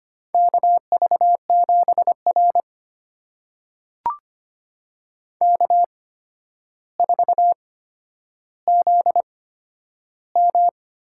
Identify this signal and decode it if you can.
Morse code, heard as a keyed tone, about 25 wpm, spaced as for 11 wpm, 700 Hz. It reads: KV7R K4ZM
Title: USA Call Signs - 4 Characters - 8x Character Spacing - 25wpm